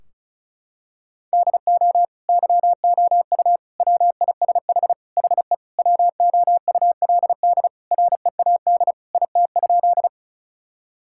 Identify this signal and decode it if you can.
Morse code, heard as a keyed tone, about 36 wpm, 700 Hz. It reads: DO YOU WISH HE WOULD READ IT?
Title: Sentences from Top 1600 Words 35wpm